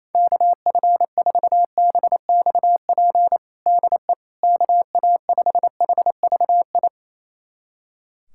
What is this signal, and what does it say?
Morse code, heard as a keyed tone, about 28 wpm, 700 Hz. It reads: KF4BXP DE KA5HVI
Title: USA Callsign DE Callsign - 6 Characters - 28wpm